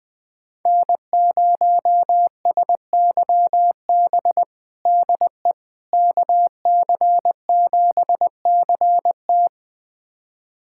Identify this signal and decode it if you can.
Morse code, heard as a keyed tone, about 20 wpm, 700 Hz. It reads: N0SYB DE KC7CT
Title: USA Callsign DE Callsign - 5 Characters - 20wpm